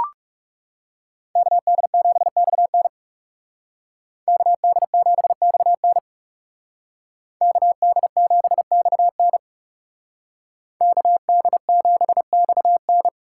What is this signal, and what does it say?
Morse code, heard as a keyed tone, about 37 wpm, 700 Hz. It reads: KD7XN KD7XN KD7XN KD7XN
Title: Speed Racing - USA 2x2 Call Signs 30wpm